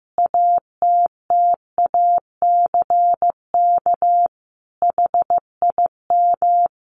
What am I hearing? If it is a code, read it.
Morse code, heard as a keyed tone, about 15 wpm, 700 Hz. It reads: ATTACK HIM
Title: Sentences from Top 1500 Words 15wpm